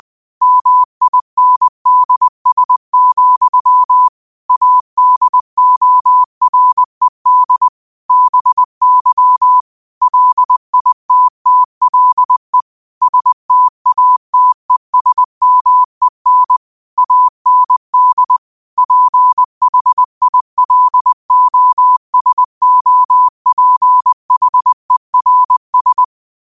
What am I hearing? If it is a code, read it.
Morse code, heard as a keyed tone, about 20 wpm, 1000 Hz. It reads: MINDS, ADORED BY LITTLE STATESMEN AND PHILOSOPHERS
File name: qotdfiles / 20wpm